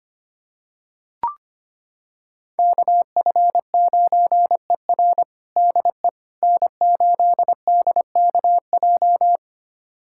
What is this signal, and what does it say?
Morse code, heard as a keyed tone, about 25 wpm, 700 Hz. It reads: KF9ER DE N8DKJ
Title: USA Callsign DE Callsign 25wpm